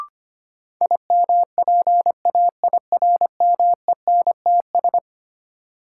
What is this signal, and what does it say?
Morse code, heard as a keyed tone, about 25 wpm, 700 Hz. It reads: IMPAIRMENTS
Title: Sets of 5 Words Encore - New Word Drill - 25wpm